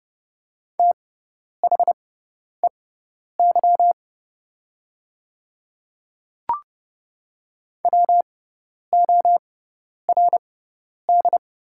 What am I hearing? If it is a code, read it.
Morse code, heard as a keyed tone, about 32 wpm, spaced as for 10 wpm, 700 Hz. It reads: THEY WORD
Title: 6x Character Spacing - Top 100 Words - 30wpm